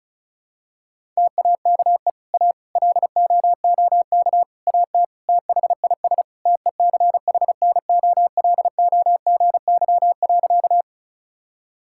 Morse code, heard as a keyed tone, about 34 wpm, 700 Hz. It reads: TAKE A LOOK AT THIS TECHNOLOGY.
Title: Sentences from Top 1000 Words - Review Part 5 - 35wpm